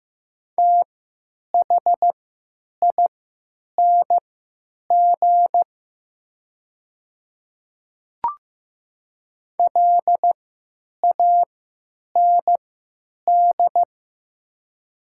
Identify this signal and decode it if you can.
Morse code, heard as a keyed tone, about 15 wpm, spaced as for 9 wpm, 700 Hz. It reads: THING LAND
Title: 3x Character Spacing - Top 200 Words - 15wpm